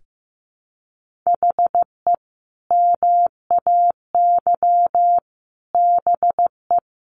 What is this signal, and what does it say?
Morse code, heard as a keyed tone, about 15 wpm, 700 Hz. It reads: HE MAY BE
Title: Sentences from Top 1400 Words 15wpm